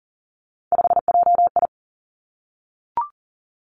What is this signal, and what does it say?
Morse code, heard as a keyed tone, about 40 wpm, 700 Hz. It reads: 5JI
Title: Three Challenging Letter-Number Combinations 40wpm